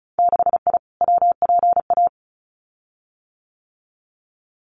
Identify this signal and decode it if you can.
Morse code, heard as a keyed tone, about 36 wpm, 700 Hz. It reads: 6I WPA
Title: Winter Field Day Exchanges Encore 35wpm